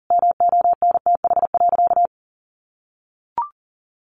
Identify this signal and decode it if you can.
Morse code, heard as a keyed tone, about 40 wpm, 700 Hz. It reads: MONTH.
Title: Sentences from Top 800 Words 40wpm